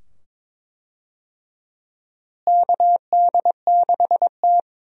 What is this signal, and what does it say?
Morse code, heard as a keyed tone, about 22 wpm, 700 Hz. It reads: KD6T
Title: USA Call Signs - 4 Characters - Repeated 2x - 22wpm